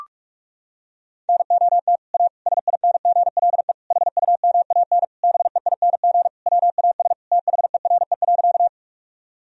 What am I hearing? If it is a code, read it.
Morse code, heard as a keyed tone, about 41 wpm, 700 Hz. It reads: NOT A SINGLE HUMAN BEING WAS THERE.
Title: Sentences from Top 1000 Words - Review Part 4 - 45wpm